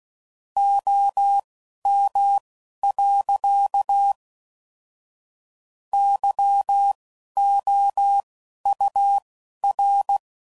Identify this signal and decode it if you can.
Morse code, heard as a keyed tone, about 16 wpm, spaced as for 11 wpm, 800 Hz. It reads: OM. YOUR
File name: CW Code Groups @10WPM / QSOs / QSO kc7sac